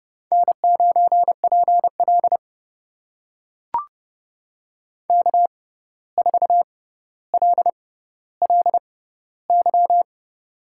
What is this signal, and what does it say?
Morse code, heard as a keyed tone, about 29 wpm, spaced as for 14 wpm, 700 Hz. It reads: N9PL K4LLY
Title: USA Call Signs - 6x Character Spacing - 30wpm